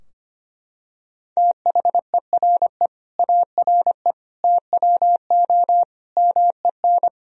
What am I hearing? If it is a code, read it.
Morse code, heard as a keyed tone, about 25 wpm, 700 Hz. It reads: THERE ARE TWO MEN
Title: Sentences from Top 1000 Words - Review Part 3 - 25wpm